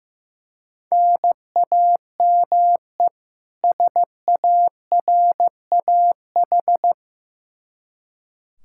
Morse code, heard as a keyed tone, about 15 wpm, 700 Hz. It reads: NAME SARAH
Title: K1USN SST QSO Practice 15wpm